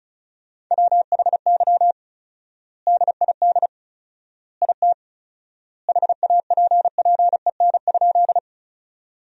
Morse code, heard as a keyed tone, about 36 wpm, 700 Hz. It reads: WHY DID IT HAPPEN?
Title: 4x Word-Spacing - Sentences from Top 400 Words - 35wpm